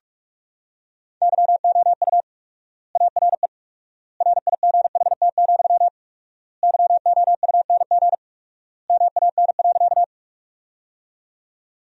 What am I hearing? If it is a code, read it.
Morse code, heard as a keyed tone, about 46 wpm, 700 Hz. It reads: YOU ARE RIGHT, YOUNG MAN.
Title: 4x Word-Spacing - Sentences from Top 400 Words - 45wpm